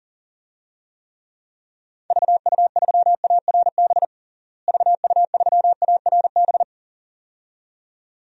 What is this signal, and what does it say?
Morse code, heard as a keyed tone, about 40 wpm, 700 Hz. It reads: VU3ARB VU3ARB
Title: DXCC-100 Call Signs - 6 Characters - Repeated 2x - 40wpm